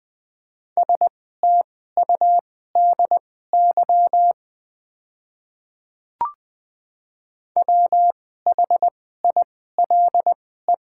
Morse code, heard as a keyed tone, about 20 wpm, spaced as for 14 wpm, 700 Hz. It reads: STUDY WHILE
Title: 2x Character Spacing - Top 300 Words - 20wpm